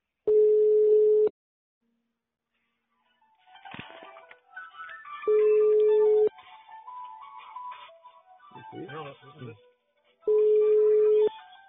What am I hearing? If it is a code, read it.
Ringback tone (Europe, 425 Hz), with speech.